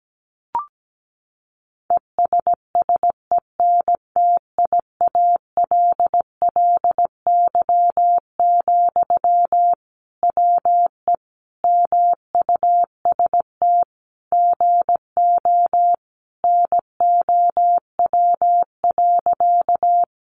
Morse code, heard as a keyed tone, about 17 wpm, 700 Hz. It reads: ESSENTIALLY, WE MUST GO NOW.
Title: Sentences from Top 1300 Words 17wpm